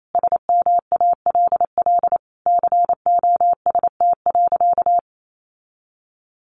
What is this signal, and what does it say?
Morse code, heard as a keyed tone, about 28 wpm, 690 Hz. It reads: SMALL COST.
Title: Sentences from Top 900 Words 28wpm